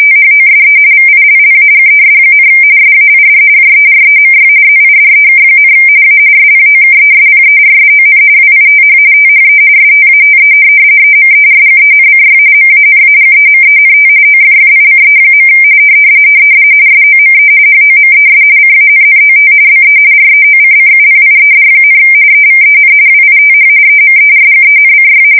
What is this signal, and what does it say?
RTTY, about 45.45 baud. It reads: THOSE VILLAGES, SAID JEREMY ZIDEK, A SPOKESPERSON WITH THE STATE EMERGENCY MANAGEMENT OFFICE. SOME HOMES CANNOT BE REOCCUPIED, EVEN WITH EMERGENCY
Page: 170 Shift 2125/2295